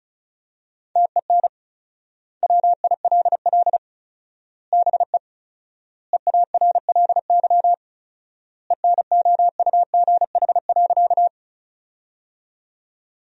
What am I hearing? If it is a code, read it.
Morse code, heard as a keyed tone, about 36 wpm, 700 Hz. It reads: TEN WILL BE EARLY ENOUGH.
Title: 4x Word-Spacing - Sentences from Top 400 Words - 35wpm